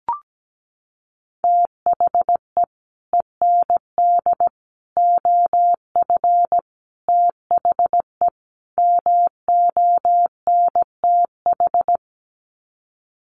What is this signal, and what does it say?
Morse code, heard as a keyed tone, about 17 wpm, 690 Hz. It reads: THE END OF THE MONTH
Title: Sets of 5 Words Encore 17wpm